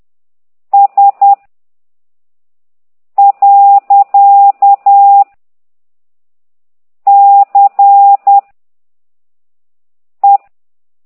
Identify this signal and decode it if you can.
Morse code, heard as a keyed tone, about 10 wpm, 800 Hz. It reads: S . C E
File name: ct2idw / exerc9